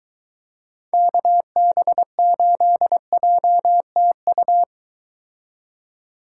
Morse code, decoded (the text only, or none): KB8JTU